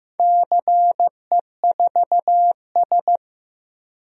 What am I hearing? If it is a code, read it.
Morse code, heard as a keyed tone, about 15 wpm, 700 Hz. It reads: CE4S
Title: DXCC-100 Call Signs - 4 Characters - Repeated 2x - 15wpm